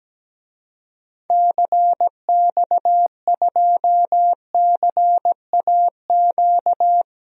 Morse code, heard as a keyed tone, about 17 wpm, 700 Hz. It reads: CX2CAQ
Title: DXCC-100 Call Signs - 6 Characters - 17wpm